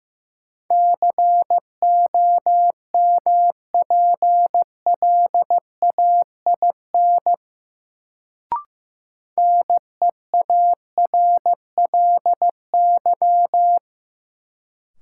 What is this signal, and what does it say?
Morse code, heard as a keyed tone, about 15 wpm, 700 Hz. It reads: COMPLAIN NEARLY
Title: Sentences from Top 1300 Words - New Word Drill Alt - 15wpm